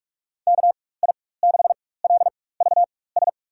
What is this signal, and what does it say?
Morse code, heard as a keyed tone, about 46 wpm, spaced as for 21 wpm, 700 Hz. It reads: KI6LVS
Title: USA Call Signs - 6 Characters - 4x Character Spacing - 45wpm